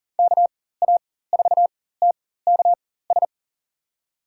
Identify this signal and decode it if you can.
Morse code, heard as a keyed tone, about 40 wpm, spaced as for 19 wpm, 700 Hz. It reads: KA4TKS
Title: USA Call Signs - 6 Characters - 4x Character Spacing - 40wpm